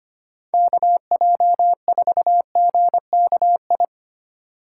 Morse code, heard as a keyed tone, about 25 wpm, 700 Hz. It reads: KJ4GKI
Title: USA Call Signs - 6 Characters - 2x Character Spacing - 25wpm